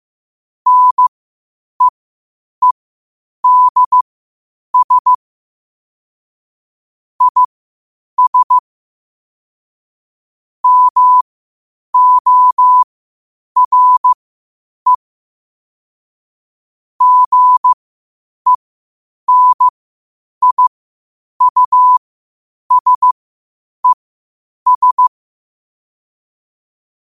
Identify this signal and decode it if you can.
Morse code, heard as a keyed tone, about 15 wpm, spaced as for 8 wpm, 1000 Hz. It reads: NEEDS IS MORE GENIUSES
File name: qotdfiles / 8wpm